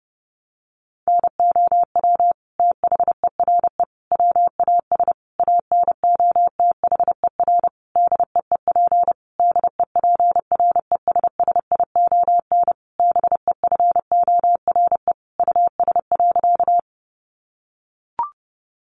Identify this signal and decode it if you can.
Morse code, heard as a keyed tone, about 30 wpm, 700 Hz. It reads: NOW THERE WAS ANOTHER DEEP DEPRESSION BEFORE US.
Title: Sentences from Top 1300 Words 30wpm